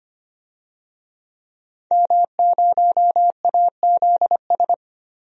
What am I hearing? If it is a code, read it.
Morse code, heard as a keyed tone, about 25 wpm, 700 Hz. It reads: M0AZS
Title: DXCC-100 Call Signs - 5 Characters - 25wpm